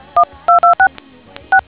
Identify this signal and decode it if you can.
DTMF, heard as telephone touch tones, tones of about 100 ms. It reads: 126 6